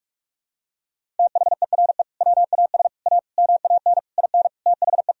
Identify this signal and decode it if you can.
Morse code, heard as a keyed tone, about 41 wpm, 700 Hz. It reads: THERE WAS A MAN IN THE
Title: Sentences from Top 1000 Words - Review Part 3 - 45wpm